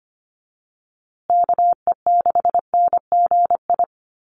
Morse code, heard as a keyed tone, about 25 wpm, 700 Hz. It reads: KE6NGI
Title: USA Call Signs Repeated 2x 25wpm